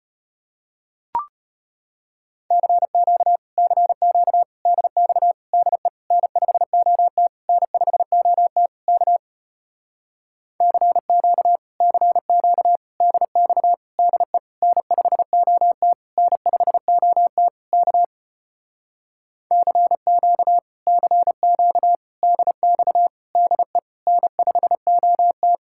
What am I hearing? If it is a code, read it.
Morse code, heard as a keyed tone, about 33 wpm, 700 Hz. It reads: CQ CQ DX DE N5OT N5OT K CQ CQ DX DE N5OT N5OT K CQ CQ DX DE N5OT